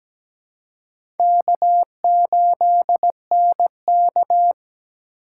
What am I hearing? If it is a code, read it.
Morse code, heard as a keyed tone, about 17 wpm, 700 Hz. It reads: K8NK
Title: USA Call Signs - 4 Characters - 2x Character Spacing - 17wpm